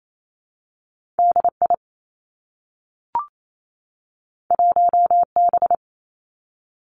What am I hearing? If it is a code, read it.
Morse code, heard as a keyed tone, about 28 wpm, 700 Hz. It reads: D I 1 B
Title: Two Challenging Letter-Number Combinations 28wpm